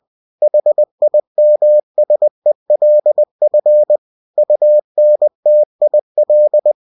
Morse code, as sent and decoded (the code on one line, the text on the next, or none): .... .. -- ... .-.. ..-. / ..- -. - .. .-..
HIMSELF UNTIL